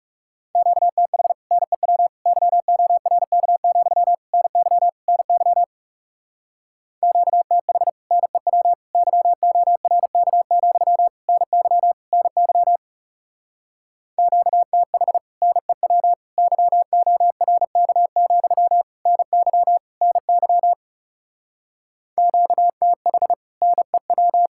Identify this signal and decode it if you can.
Morse code, heard as a keyed tone, about 36 wpm, 700 Hz. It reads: QTH NEW YORK, NY NY QTH NEW YORK, NY NY QTH NEW YORK, NY NY QTH NEW
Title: Speed-Racing Common QSO Phrases Encore - 30wpm